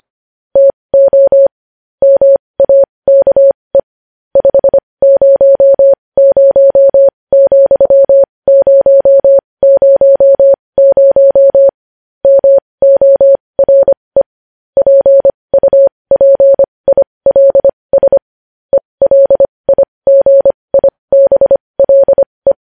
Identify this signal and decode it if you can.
Morse code, heard as a keyed tone, about 25 wpm, spaced as for 20 wpm, 550 Hz. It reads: TO MAKE 500,000 MORE PUPILS ELIGIBLE
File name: podcast / News headlines-1x-25x20